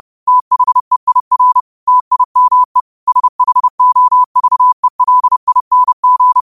Morse code, heard as a keyed tone, about 30 wpm, 1000 Hz. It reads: THEIR TIME SHOVELING
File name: qotdfiles / 30wpm